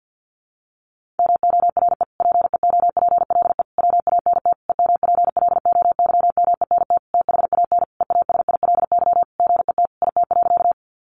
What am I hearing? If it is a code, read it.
Morse code, heard as a keyed tone, about 50 wpm, 700 Hz. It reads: MORE PEOPLE WANT EMPLOYMENT THAN EASILY GET IT.
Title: Sentences from Top 1500 Words 50wpm